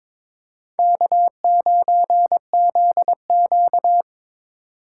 Morse code, heard as a keyed tone, about 22 wpm, 700 Hz. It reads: K9ZQ
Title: USA Call Signs - 3x Character Spacing - 22wpm